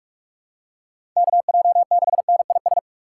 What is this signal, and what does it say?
Morse code, heard as a keyed tone, about 42 wpm, 700 Hz. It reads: KJ6NIS